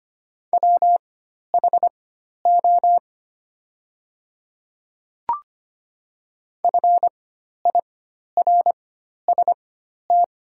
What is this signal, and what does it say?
Morse code, heard as a keyed tone, about 25 wpm, spaced as for 12 wpm, 700 Hz. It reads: WHO FIRST
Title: 4x Character Spacing - Top 300 Words - 25wpm